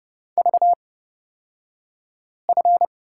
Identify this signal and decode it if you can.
Morse code, heard as a keyed tone, about 30 wpm, spaced as for 5 wpm, 700 Hz. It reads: VF
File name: Lesson 37d - Letter x - Two Characters 30wpm